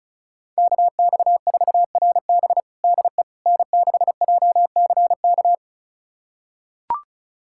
Morse code, heard as a keyed tone, about 36 wpm, 700 Hz. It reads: KX4RB DE N6JCK
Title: USA Callsign DE Callsign - 5 Characters - 35wpm